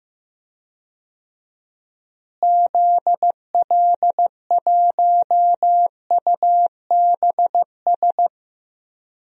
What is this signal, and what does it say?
Morse code, heard as a keyed tone, about 15 wpm, 700 Hz. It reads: ZL1UBS
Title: DXCC-100 Call Signs - 6 Characters - Repeated 3x - 15wpm